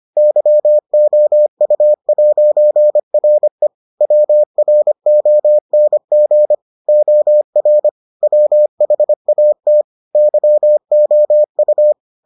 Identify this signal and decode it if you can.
Morse code, heard as a keyed tone, about 25 wpm, 600 Hz. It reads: YOU'RE WRONG OR WHAT YOU